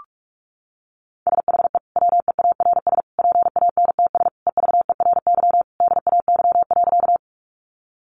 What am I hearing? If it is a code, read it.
Morse code, heard as a keyed tone, about 44 wpm, 700 Hz. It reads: SHE WEARS PANTS EVERY DAY.
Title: Additional Easy Phrases 45wpm